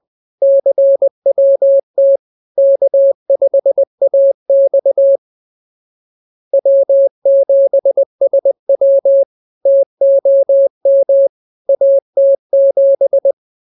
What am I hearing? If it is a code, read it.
Morse code, heard as a keyed tone, about 20 wpm, 550 Hz. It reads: CWT K5AX W7SW TOM AT7
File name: CWT-206-20